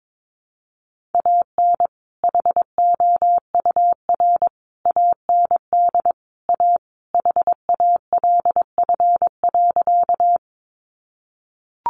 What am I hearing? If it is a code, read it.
Morse code, heard as a keyed tone, about 22 wpm, 700 Hz. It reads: AN HOUR AND A HALF.
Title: Sentences from Top 400 Words 22wpm